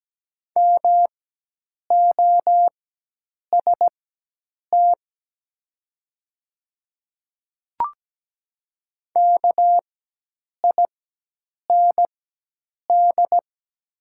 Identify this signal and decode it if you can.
Morse code, heard as a keyed tone, about 17 wpm, spaced as for 8 wpm, 700 Hz. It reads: MOST KIND
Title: 4x Character Spacing - Top 200 Words - 17wpm